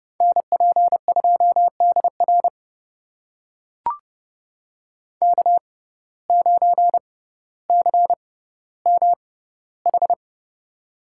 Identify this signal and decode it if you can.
Morse code, heard as a keyed tone, about 28 wpm, spaced as for 15 wpm, 700 Hz. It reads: NP2DR K9CMH
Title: USA Call Signs - 5 Characters - 6x Character Spacing - 30wpm